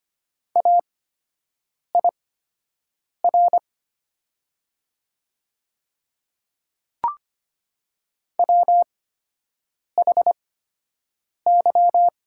Morse code, heard as a keyed tone, about 25 wpm, spaced as for 7 wpm, 700 Hz. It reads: AIR WHY
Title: 8x Character Spacing - Top 200 Words - 25wpm